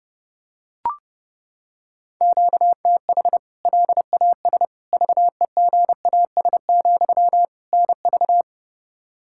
Morse code, heard as a keyed tone, about 30 wpm, 700 Hz. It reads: QTH LAS VEGAS, NV
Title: K1USN SST QSO Practice 30wpm